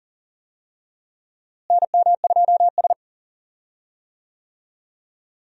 Morse code, heard as a keyed tone, about 40 wpm, 700 Hz. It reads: NM2S
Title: USA Call Signs - 4 Characters - Repeated 3x - 40wpm